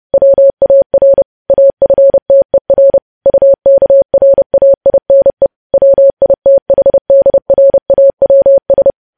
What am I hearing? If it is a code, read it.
Morse code, heard as a keyed tone, about 30 wpm, 550 Hz. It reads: WAR AFTER UKRAINE WITHDRAWS
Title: News Headlines at 30